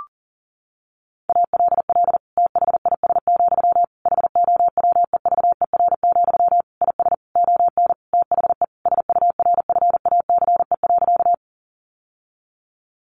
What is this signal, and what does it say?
Morse code, heard as a keyed tone, about 40 wpm, 700 Hz. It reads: ALL THIS, HOWEVER, IS ON THE SURFACE.
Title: Sentences from Top 600 Words 40wpm